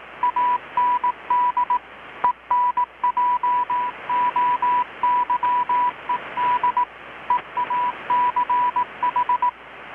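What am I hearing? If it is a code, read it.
Morse code, heard as a keyed tone, about 18 wpm, 980 Hz. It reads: AND ENJOYED EACH
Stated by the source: Morse from KPH's Night of Nights XX maritime radiotelegraphy event, received in Chiapas, Mexico about 3,800 km away on a Kenwood R-600 with a grounded 10 m long-wire antenna.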